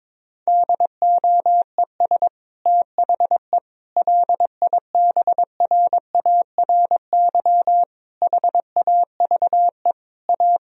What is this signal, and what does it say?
Morse code, heard as a keyed tone, about 22 wpm, 700 Hz. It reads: DOES THE LIBRARY HAVE A
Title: Sentences from Top 1000 Words - Review Part 4 - 22wpm